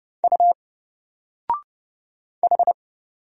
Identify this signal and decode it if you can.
Morse code, heard as a keyed tone, about 30 wpm, 700 Hz. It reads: U H
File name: Single-Letter-Number-30wpm